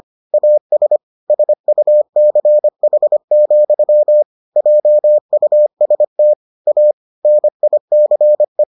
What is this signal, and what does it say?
Morse code, heard as a keyed tone, about 25 wpm, 600 Hz. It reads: AS SUCH, JUST A NICE